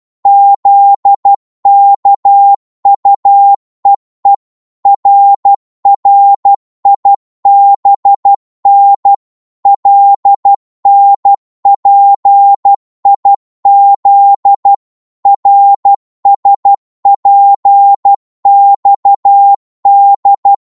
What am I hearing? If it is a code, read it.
Morse code, heard as a keyed tone, about 12 wpm, 800 Hz. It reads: ZKUEE RRIBN LNPIZ RSPXD
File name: Morse / ltr-12wpm